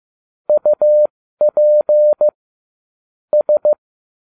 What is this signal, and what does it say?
Morse code, heard as a keyed tone, about 15 wpm, spaced as for 13 wpm, 600 Hz. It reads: UP S